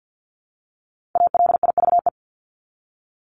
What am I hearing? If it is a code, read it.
Morse code, heard as a keyed tone, about 48 wpm, 700 Hz. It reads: ALIVE
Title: Top 1751-2000 Words 50wpm